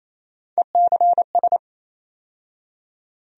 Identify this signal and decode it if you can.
Morse code, heard as a keyed tone, about 27 wpm, 700 Hz. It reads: ECS
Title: USA Call Sign Postfixes - 3 Letters 28wpm